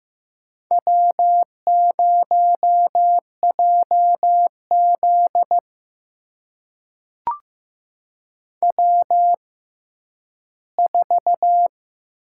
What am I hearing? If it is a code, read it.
Morse code, heard as a keyed tone, about 15 wpm, spaced as for 10 wpm, 700 Hz. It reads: W0JZ W4
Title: USA Call Signs - 6x Character Spacing - 15wpm